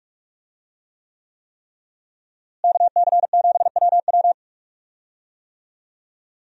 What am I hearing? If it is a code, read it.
Morse code, heard as a keyed tone, about 44 wpm, 700 Hz. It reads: KC7WW
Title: USA Call Signs - 5 Characters - Repeated 3x - 45wpm